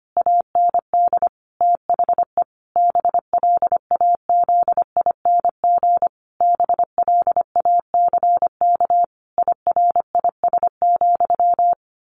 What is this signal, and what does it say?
Morse code, heard as a keyed tone, about 25 wpm, 700 Hz. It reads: AND THE BLAZING BLACK IRIS,
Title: A Sound Of Thunder 25wpm